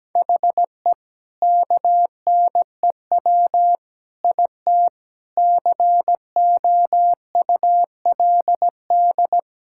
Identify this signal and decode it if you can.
Morse code, heard as a keyed tone, about 17 wpm, 700 Hz. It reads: HE KNEW IT COULD
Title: Sentences from Top 1000 Words - Review Part 6 - 17wpm